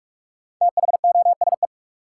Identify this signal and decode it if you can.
Morse code, heard as a keyed tone, about 46 wpm, 700 Hz. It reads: THOSE
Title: Sentences from Top 300 Words - New Word Drill - 45wpm